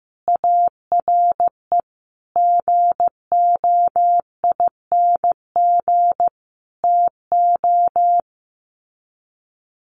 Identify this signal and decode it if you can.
Morse code, heard as a keyed tone, about 15 wpm, 700 Hz. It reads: ARE GOING TO
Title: Sets of 4 Words 15wpm